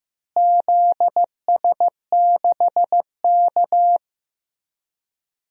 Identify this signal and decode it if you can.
Morse code, heard as a keyed tone, about 15 wpm, 700 Hz. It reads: ZS6K
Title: DXCC-100 Call Signs - 4 Characters - Repeated 2x - 15wpm